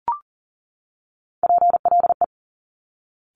Morse code, heard as a keyed tone, about 42 wpm, 700 Hz. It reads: PLE
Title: Top Three Letter Combinations in Words 40wpm